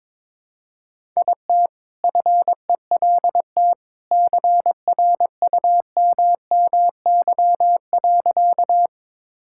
Morse code, heard as a keyed tone, about 22 wpm, 700 Hz. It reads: IT FELT CRUMMY.
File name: All-Age Appropriate Jokes - 22wpm